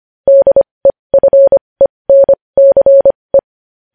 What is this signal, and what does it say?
Morse code, heard as a keyed tone, about 25 wpm, spaced as for 20 wpm, 550 Hz. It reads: DEFENCE
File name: podcast / News headlines-1x-25x20